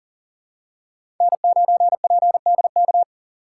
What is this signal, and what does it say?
Morse code, heard as a keyed tone, about 40 wpm, 700 Hz. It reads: N9PDK
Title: USA Call Signs - 5 Characters - 3x Character Spacing - 40wpm